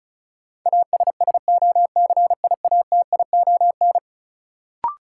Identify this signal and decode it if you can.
Morse code, heard as a keyed tone, about 34 wpm, 700 Hz. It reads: ASSOCIATION
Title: Sentences from Top 1100 Words - New Word Drill Alt - 35wpm